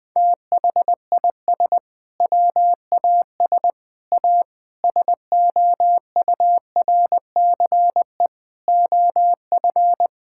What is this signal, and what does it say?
Morse code, heard as a keyed tone, about 20 wpm, 700 Hz. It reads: THIS WAS A SOURCE OF
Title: Sentences from Top 1000 Words - Review Part 5 - 20wpm